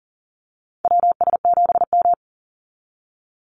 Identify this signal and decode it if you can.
Morse code, heard as a keyed tone, about 40 wpm, 700 Hz. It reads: WS7M
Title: Call Signs - 40wpm